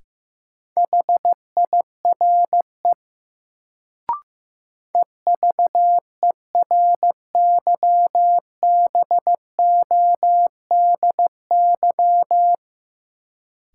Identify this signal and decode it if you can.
Morse code, heard as a keyed tone, about 15 wpm, 700 Hz. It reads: HIRE EVERYBODY
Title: Sentences from Top 1400 Words - New Word Drill - 15wpm